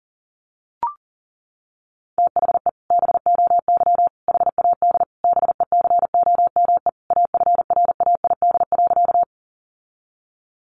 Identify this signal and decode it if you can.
Morse code, heard as a keyed tone, about 40 wpm, 700 Hz. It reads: THE BOY HAD BECOME AFRAID.
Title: Sentences from Top 1500 Words 40wpm